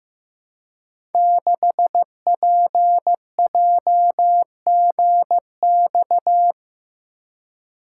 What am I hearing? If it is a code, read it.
Morse code, heard as a keyed tone, about 15 wpm, 700 Hz. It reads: 6PJGX
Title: Code Groups - 5 Letter:Numbers - 15wpm